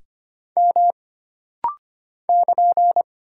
Morse code, heard as a keyed tone, about 25 wpm, 700 Hz. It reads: M <KN>
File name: Single Character (Letter:Number:Punctuation:Prosign) - Rapid-Fire - 25wpm